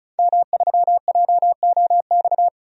Morse code, heard as a keyed tone, about 34 wpm, 700 Hz. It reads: M3JOX
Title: DXCC-100 Call Signs - 5 Characters - 35wpm